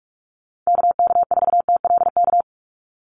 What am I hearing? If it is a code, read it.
Morse code, heard as a keyed tone, about 44 wpm, 700 Hz. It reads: KK4TLK
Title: USA Call Signs 45wpm